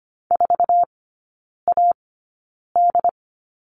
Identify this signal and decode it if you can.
Morse code, heard as a keyed tone, about 25 wpm, spaced as for 9 wpm, 700 Hz. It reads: 4AD
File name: Lesson 14e - Number 2 - Three Characters 25wpm